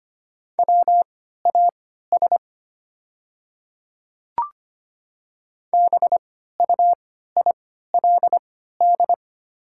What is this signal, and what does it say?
Morse code, heard as a keyed tone, about 25 wpm, spaced as for 14 wpm, 700 Hz. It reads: WAS BUILD
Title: 3x Character Spacing - Top 300 Words - 25wpm